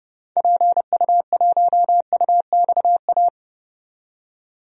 Morse code, heard as a keyed tone, about 30 wpm, 700 Hz. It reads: PU1UXA
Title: DXCC-100 Call Signs - 6 Characters - 30wpm